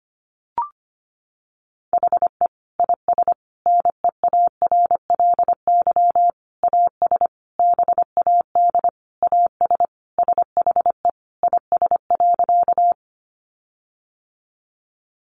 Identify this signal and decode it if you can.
Morse code, heard as a keyed tone, about 25 wpm, 700 Hz. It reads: HE IS NEARLY AS BAD AS SHE IS.
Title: Sentences from Top 1300 Words 25wpm